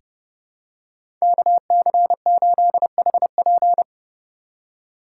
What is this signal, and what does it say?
Morse code, heard as a keyed tone, about 30 wpm, 700 Hz. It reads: KC8HP
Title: USA Call Signs - 5 Characters - 8x Character Spacing - 30wpm